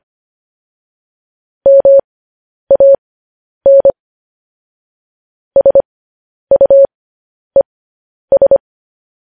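Morse code, heard as a keyed tone, about 25 wpm, spaced as for 10 wpm, 550 Hz. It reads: MAN SUES
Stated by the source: Morse code news headlines broadcast.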